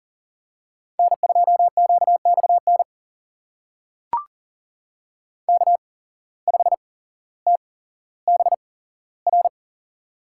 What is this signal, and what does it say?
Morse code, heard as a keyed tone, about 40 wpm, spaced as for 17 wpm, 700 Hz. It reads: N2QXN K5TBR
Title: USA Call Signs - 5 Characters - 8x Character Spacing - 40wpm